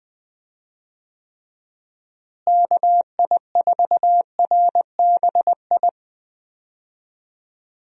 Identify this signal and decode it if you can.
Morse code, heard as a keyed tone, about 21 wpm, 700 Hz. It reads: KI4RBI
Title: USA Call Signs - 6 Characters - Repeated 2x - 20wpm